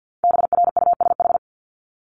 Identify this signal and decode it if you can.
Morse code, heard as a keyed tone, about 50 wpm, 700 Hz. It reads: BRUSH
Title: Top 1001-1250 Words 50wpm